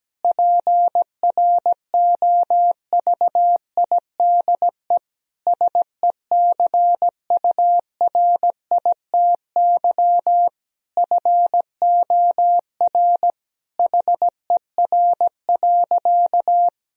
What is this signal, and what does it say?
Morse code, heard as a keyed tone, about 17 wpm, 700 Hz. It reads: PROVIDE SECURITY FOR HER.
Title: Sentences from Top 1000 Words - Review Part 4 - 17wpm